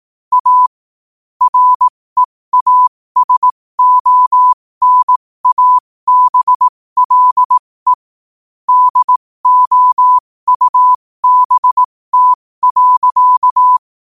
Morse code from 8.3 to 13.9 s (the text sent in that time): DOUBT.